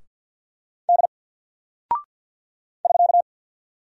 Morse code, heard as a keyed tone, about 50 wpm, 700 Hz. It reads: D <SK>
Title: Single Character (Letter:Number:Punctuation:Prosign) 50wpm